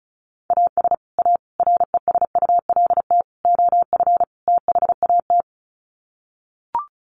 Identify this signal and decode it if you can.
Morse code, heard as a keyed tone, about 34 wpm, 700 Hz. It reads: AS A RESULT OF THAT
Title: Sets of 5 Words Encore 35wpm